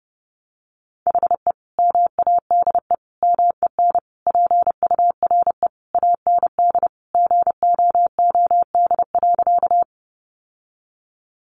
Morse code, heard as a keyed tone, about 30 wpm, 700 Hz. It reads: HE MADE MEN PURE AND GOOD.
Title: Sentences from Top 1500 Words 30wpm